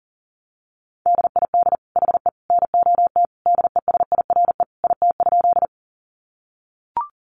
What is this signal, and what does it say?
Morse code, heard as a keyed tone, about 40 wpm, 700 Hz. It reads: DID HE NOT DESIRE IT?
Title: Sentences from Top 1100 Words 40wpm